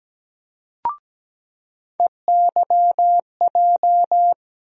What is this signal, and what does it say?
Morse code, heard as a keyed tone, about 17 wpm, 700 Hz. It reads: EYJ
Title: USA Call Sign Postfixes - 3 Letters 17wpm